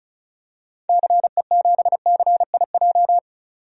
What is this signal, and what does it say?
Morse code, heard as a keyed tone, about 34 wpm, 700 Hz. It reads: CE7CIJ
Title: DXCC-100 Call Signs - 6 Characters - 35wpm